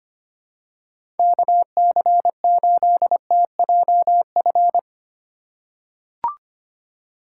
Morse code, heard as a keyed tone, about 25 wpm, 700 Hz. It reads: KC8TJF
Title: USA Call Signs - 6 Characters - 3x Character Spacing - 25wpm